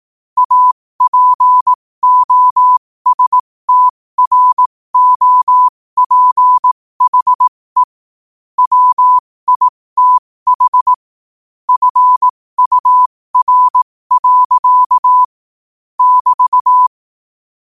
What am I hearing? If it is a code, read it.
Morse code, heard as a keyed tone, about 18 wpm, spaced as for 15 wpm, 1000 Hz. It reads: APOSTROPHE WITH FUR. <BT>
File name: qotdfiles / 15wpm